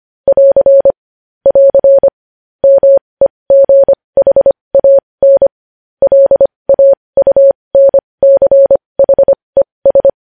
Morse code, heard as a keyed tone, about 25 wpm, spaced as for 20 wpm, 550 Hz. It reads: <AR> <AR> MEGHAN LAUNCHES
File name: podcast / News headlines-1x-25x20